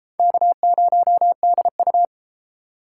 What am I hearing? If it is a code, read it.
Morse code, heard as a keyed tone, about 33 wpm, 700 Hz. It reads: K0DU